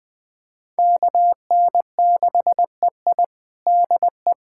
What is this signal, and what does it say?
Morse code, heard as a keyed tone, about 20 wpm, 700 Hz. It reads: KN6EI DE
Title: USA Callsign DE Callsign - 5 Characters - 20wpm